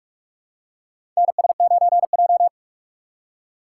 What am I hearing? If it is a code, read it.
Morse code, heard as a keyed tone, about 43 wpm, 700 Hz. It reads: NS9J
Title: USA Call Signs - 2x Character Spacing - 45wpm